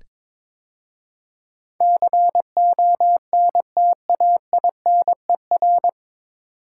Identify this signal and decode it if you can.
Morse code, heard as a keyed tone, about 22 wpm, 700 Hz. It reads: CONTAINER
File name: 9 Letter Words - First 3 Letters Given - 22wpm